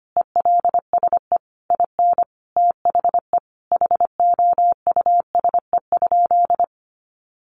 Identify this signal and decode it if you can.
Morse code, heard as a keyed tone, about 25 wpm, 700 Hz. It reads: ELSE IN THE HOUSE?
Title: Sentences from Top 900 Words 25wpm